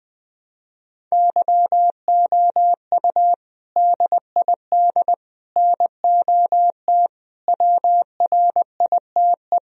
Morse code, heard as a keyed tone, about 20 wpm, 700 Hz. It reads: YOU DID NOT WRITE
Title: Sentences from Top 1000 Words - Review Part 1 - 20wpm